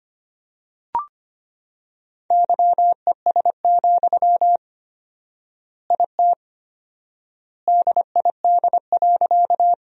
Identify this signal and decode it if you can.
Morse code, heard as a keyed tone, about 25 wpm, 700 Hz. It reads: YES, IT DID.
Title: 4x Word-Spacing - Sentences from Top 500 Words - 25wpm